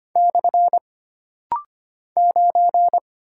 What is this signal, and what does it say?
Morse code, heard as a keyed tone, about 25 wpm, 700 Hz. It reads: / 9
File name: Single Character (Letter:Number:Punctuation:Prosign) - Rapid-Fire - 25wpm